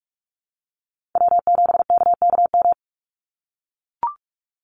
Morse code, heard as a keyed tone, about 44 wpm, 700 Hz. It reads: W7KKM
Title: NA SOTA Chaser Callsigns 45wpm